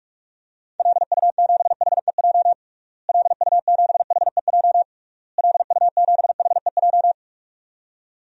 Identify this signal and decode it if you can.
Morse code, heard as a keyed tone, about 46 wpm, 700 Hz. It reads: LU7HEJ LU7HEJ LU7HEJ
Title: DXCC-100 Call Signs - 6 Characters - Repeated 3x - 45wpm